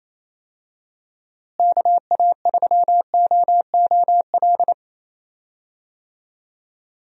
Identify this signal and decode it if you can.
Morse code, heard as a keyed tone, about 27 wpm, 700 Hz. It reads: KA3OOL
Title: USA Call Signs - 6 Characters - Repeated 3x - 28wpm